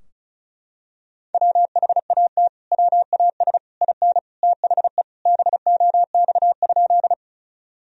Morse code, heard as a keyed tone, about 34 wpm, 700 Hz. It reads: WHAT WAS IN THE BOX?
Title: Sentences from Top 1000 Words - Review Part 2 - 35wpm